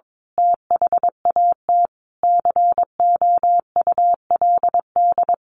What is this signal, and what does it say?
Morse code, heard as a keyed tone, about 22 wpm, 700 Hz. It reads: THAT COULD
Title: A Sound Of Thunder 22wpm